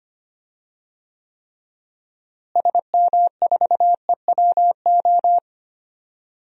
Morse code, heard as a keyed tone, about 25 wpm, 700 Hz. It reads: SM4EWO
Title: DXCC-100 Call Signs - 6 Characters - Repeated 3x - 25wpm